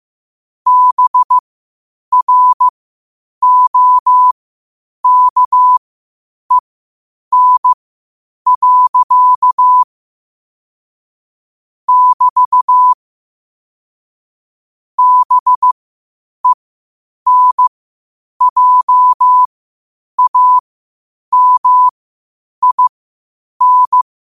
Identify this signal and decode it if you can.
Morse code, heard as a keyed tone, about 15 wpm, spaced as for 8 wpm, 1000 Hz. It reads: BROKEN. <BT> BENJAMIN
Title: 8 WPM morse code quotes for Thu, 29 Feb 2024